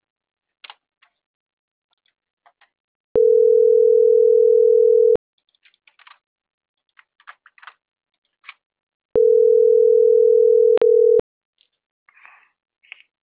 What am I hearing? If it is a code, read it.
Ringback tone (North America), 440 + 480 Hz.